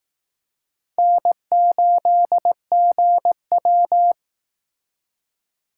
Morse code, heard as a keyed tone, about 18 wpm, 700 Hz. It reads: N8GW
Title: Speed Racing - USA 1x2 Call Signs 15wpm